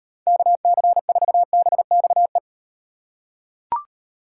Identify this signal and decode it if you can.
Morse code, heard as a keyed tone, about 38 wpm, 700 Hz. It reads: KC4BXE